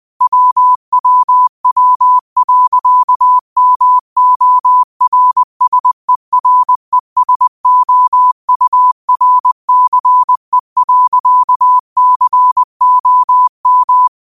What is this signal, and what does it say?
Morse code, heard as a keyed tone, about 20 wpm, 1000 Hz. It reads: WWW.MORSERESOURCE.COM